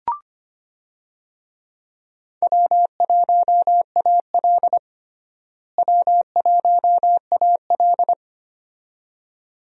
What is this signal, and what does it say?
Morse code, heard as a keyed tone, about 25 wpm, 700 Hz. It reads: W1AL W1AL
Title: USA Call Signs - 4 Characters - Repeated 2x - 25wpm